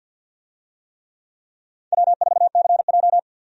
Morse code, heard as a keyed tone, about 52 wpm, 700 Hz. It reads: W4CJ
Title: USA Call Signs - 4 Characters - 50wpm